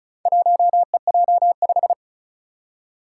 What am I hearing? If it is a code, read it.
Morse code, heard as a keyed tone, about 36 wpm, 700 Hz. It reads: 1EJ5
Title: Four Challenging Letter-Number Combinations 35wpm